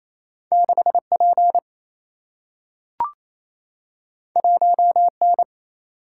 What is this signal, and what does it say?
Morse code, heard as a keyed tone, about 28 wpm, 700 Hz. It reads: 6P 1N
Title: Single-Letter Single-Number 28wpm